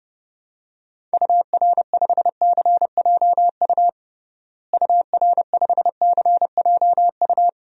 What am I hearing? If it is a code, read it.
Morse code, heard as a keyed tone, about 29 wpm, 700 Hz. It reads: UR5CJU UR5CJU
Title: DXCC-100 Call Signs - 6 Characters - Repeated 2x - 30wpm